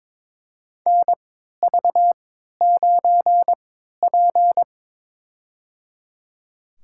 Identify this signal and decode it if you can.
Morse code, heard as a keyed tone, about 21 wpm, spaced as for 12 wpm, 700 Hz. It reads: NV9P
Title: USA Call Signs - 3x Character Spacing - 22wpm